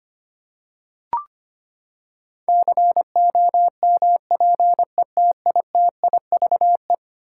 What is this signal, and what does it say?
Morse code, heard as a keyed tone, about 25 wpm, 700 Hz. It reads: COMPETITIVE
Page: morse-code-ninja-practice-25wpm